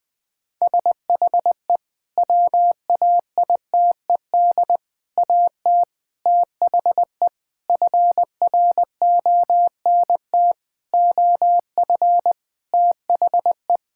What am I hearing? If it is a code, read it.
Morse code, heard as a keyed tone, about 20 wpm, 700 Hz. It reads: SHE WAITED AT THE FRONT OF THE
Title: Sentences with 5-Word Phrases - Spoken Beginning - 20wpm